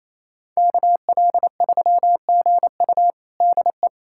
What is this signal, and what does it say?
Morse code, heard as a keyed tone, about 28 wpm, 700 Hz. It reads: KL3GU DE
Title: USA Callsign DE Callsign - 5 Characters - 28wpm